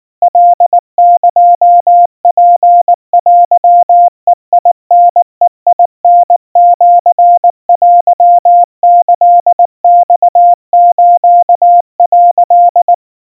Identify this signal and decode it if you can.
Morse code (Japanese Wabun code), heard as a keyed tone, about 19 wpm, 700 Hz. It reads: カエツテベタベタシテキマス」
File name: kawa / Morse sample / kamyu 70 cpm 700 Hz Crispy